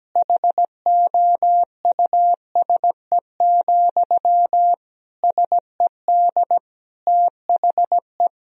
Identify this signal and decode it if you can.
Morse code, heard as a keyed tone, about 17 wpm, 700 Hz. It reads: HOUSE, SED THE
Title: Sentences from Top 400 Words and Abbreviations 17wpm